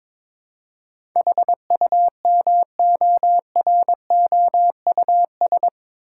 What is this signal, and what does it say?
Morse code, heard as a keyed tone, about 22 wpm, 700 Hz. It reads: HUMOROUS
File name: practice / 22wpm / Eight Letter Words - 22wpm